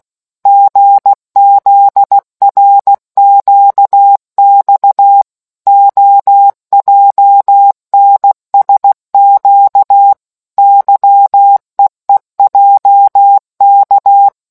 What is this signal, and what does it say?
Morse code, heard as a keyed tone, about 16 wpm, 800 Hz. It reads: GZRQX OJNSQ YEEJK